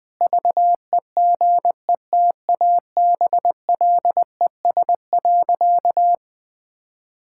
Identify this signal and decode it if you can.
Morse code, heard as a keyed tone, about 20 wpm, 700 Hz. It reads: VEGETABLES.
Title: Pangram Challenge v2 - 20wpm